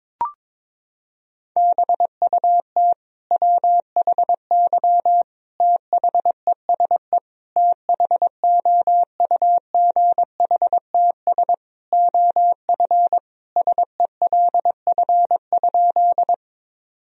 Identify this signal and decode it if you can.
Morse code, heard as a keyed tone, about 22 wpm, 700 Hz. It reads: BUT WHY THESE THOUGHTS OF SELF?
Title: Sentences from Top 1000 Words - Review Part 1 - 22wpm